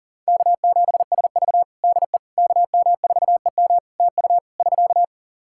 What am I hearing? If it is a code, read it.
Morse code, heard as a keyed tone, about 40 wpm, 700 Hz. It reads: K7SV DE KM4EM TU <SK>
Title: Practice QSO - Exchange 4 - Ending - 40wpm